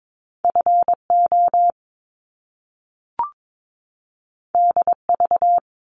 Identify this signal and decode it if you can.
Morse code, heard as a keyed tone, about 22 wpm, 700 Hz. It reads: FO DV
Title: Top Two Letter Combinations in Words 22wpm